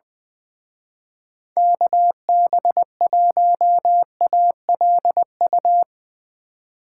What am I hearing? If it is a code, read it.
Morse code, heard as a keyed tone, about 21 wpm, 700 Hz. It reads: KB1ALU
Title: Speed Racing - USA 2x3 Call Signs 20wpm